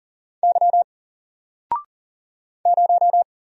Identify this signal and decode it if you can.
Morse code, heard as a keyed tone, about 40 wpm, 700 Hz. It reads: Y 0
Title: Single Character (Letter:Number:Punctuation:Prosign) 40wpm